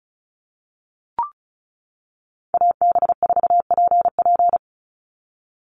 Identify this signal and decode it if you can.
Morse code, heard as a keyed tone, about 34 wpm, 700 Hz. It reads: AB4PP
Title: NA SOTA Chaser Callsigns 35wpm